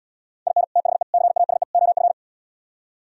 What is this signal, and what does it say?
Morse code, heard as a keyed tone, about 77 wpm, 700 Hz. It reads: IN THE CASE OF